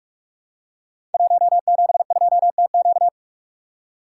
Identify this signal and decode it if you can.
Morse code, heard as a keyed tone, about 44 wpm, 700 Hz. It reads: 172TQ
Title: Code Groups - 5 Letter:Numbers - 45wpm